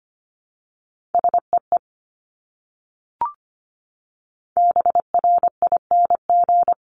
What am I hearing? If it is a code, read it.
Morse code, heard as a keyed tone, about 25 wpm, 700 Hz. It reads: SEE BRING
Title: Top 500 Words Repeated 25wpm